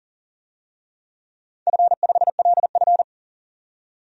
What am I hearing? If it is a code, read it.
Morse code, heard as a keyed tone, about 39 wpm, 700 Hz. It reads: F5LF
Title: DXCC-100 Call Signs - 4 Characters - 40wpm